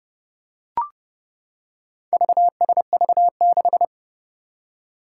Morse code, heard as a keyed tone, about 30 wpm, 700 Hz. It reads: VSV6
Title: Four Challenging Letter-Number Combinations 30wpm